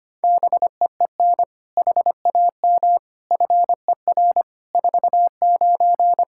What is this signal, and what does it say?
Morse code, heard as a keyed tone, about 25 wpm, 700 Hz. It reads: BEEN HAM FER 49
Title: Practice QSO - Exchange 3 - More Info - 25wpm